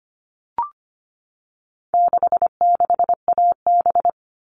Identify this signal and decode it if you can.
Morse code, heard as a keyed tone, about 25 wpm, 700 Hz. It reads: 66AB
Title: All Cut Numbers - 4-Digit 25wpm